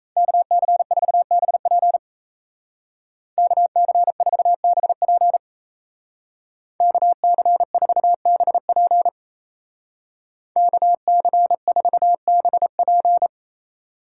Morse code, heard as a keyed tone, about 34 wpm, 700 Hz. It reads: KC4BP KC4BP KC4BP KC4BP
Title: Speed Racing - USA 2x2 Call Signs 28wpm